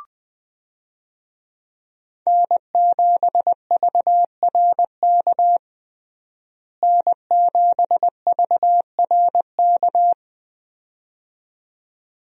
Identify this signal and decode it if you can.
Morse code, heard as a keyed tone, about 20 wpm, 700 Hz. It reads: N7VRK N7VRK
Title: USA Call Signs - 5 Characters - Repeated 2x - 20wpm